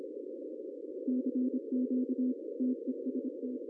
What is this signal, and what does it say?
Morse code, heard as a keyed tone, about 26 wpm, 260 Hz. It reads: CQ TEST
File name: CW / cqww-cw-k5tr